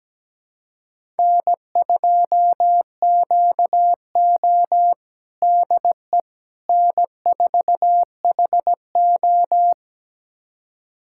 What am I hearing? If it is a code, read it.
Morse code, heard as a keyed tone, about 17 wpm, 700 Hz. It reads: N2QO DE N4HO
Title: USA Callsign DE Callsign - 4 Characters - 17wpm